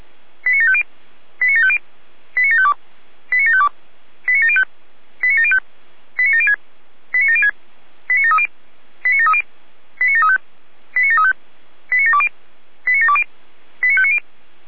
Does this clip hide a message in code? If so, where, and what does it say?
Yes: five-tone selective call (ZVEI) from 0.5 to 14.2 s, heard as quick runs of five tones, tones of about 80 ms. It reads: 89750 89750 89752 89752 89705 89705 89706 89706 89730 89730 89735 89735 89720 89720 89590